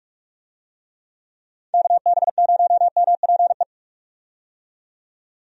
Morse code, heard as a keyed tone, about 46 wpm, 700 Hz. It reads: KB0MPE
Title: USA Call Signs - 6 Characters - 45wpm